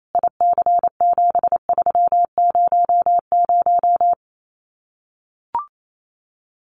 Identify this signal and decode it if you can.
Morse code, heard as a keyed tone, about 28 wpm, 700 Hz. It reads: IC7300
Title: Common QSO Phrases Encore 28wpm